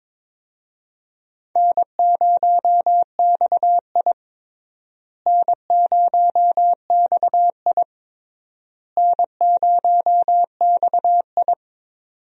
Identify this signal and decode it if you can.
Morse code, heard as a keyed tone, about 22 wpm, 700 Hz. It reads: N0XI N0XI N0XI
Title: USA Call Signs - 4 Characters - Repeated 3x - 22wpm